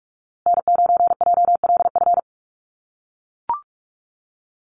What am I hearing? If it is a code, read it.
Morse code, heard as a keyed tone, about 44 wpm, 700 Hz. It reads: N9JLF
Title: USA Call Signs Repeated 2x 45wpm